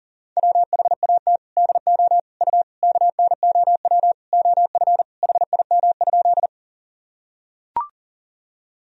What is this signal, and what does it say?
Morse code, heard as a keyed tone, about 37 wpm, 700 Hz. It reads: WHAT DO U KNOW OF HIM?
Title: Sentences from Top 100 Words and Abbreviations 40wpm